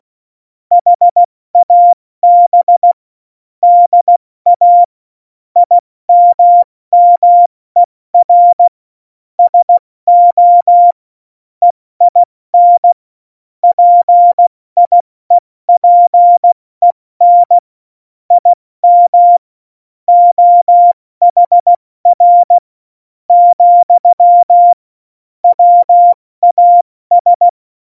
Morse code, heard as a keyed tone, about 16 wpm, 700 Hz. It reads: HAB DA IMMER SO EIN PIEPEN IM OHR , WAS
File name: Morsekurs / Witze / Witz 13 March 2018, 70 F80